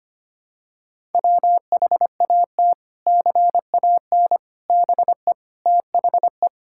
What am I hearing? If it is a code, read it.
Morse code, heard as a keyed tone, about 25 wpm, 700 Hz. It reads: WHAT CAN BE THE
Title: Sentences from Top 1000 Words - Review Part 6 - 25wpm